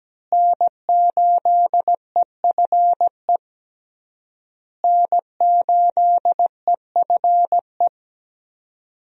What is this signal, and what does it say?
Morse code, heard as a keyed tone, about 17 wpm, 700 Hz. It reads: N8EFE N8EFE
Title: USA Call Signs - 5 Characters - Repeated 3x - 17wpm